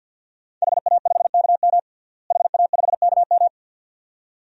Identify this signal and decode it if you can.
Morse code, heard as a keyed tone, about 49 wpm, 700 Hz. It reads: HA5KM HA5KM
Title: DXCC-100 Call Signs - 5 Characters - Repeated 2x - 50wpm